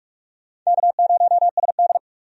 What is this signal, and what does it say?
Morse code, heard as a keyed tone, about 46 wpm, 700 Hz. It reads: K0SD
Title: USA Call Signs - 4 Characters - 45wpm